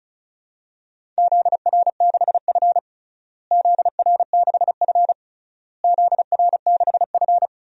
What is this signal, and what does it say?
Morse code, heard as a keyed tone, about 36 wpm, 700 Hz. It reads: ZR6F ZR6F ZR6F
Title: DXCC-100 Call Signs - 4 Characters - Repeated 3x - 35wpm